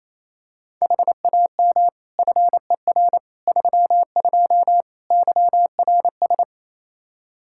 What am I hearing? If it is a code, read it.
Morse code, heard as a keyed tone, about 27 wpm, 700 Hz. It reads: HAM FER 32 YRS
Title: Practice QSO - Exchange 3 - More Info - 28wpm